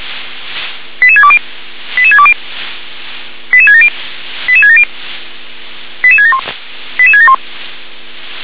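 Five-tone selective call (ZVEI), heard as quick runs of five tones, tones of about 70 ms. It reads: 88620 88620 88670 88670 88671 88671
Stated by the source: Five-tone ZVEI selective call 88775 received on an SDR.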